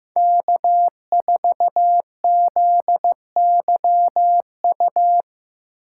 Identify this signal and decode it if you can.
Morse code, heard as a keyed tone, about 15 wpm, 700 Hz. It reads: K4ZYU
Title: USA Call Signs - 5 Characters - 6x Character Spacing - 15wpm